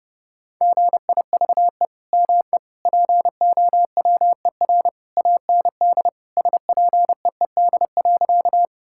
Morse code, heard as a keyed tone, about 30 wpm, 700 Hz. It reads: GIVE ME POWER AND SPEED.
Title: Sentences from Top 1000 Words - Review Part 5 - 30wpm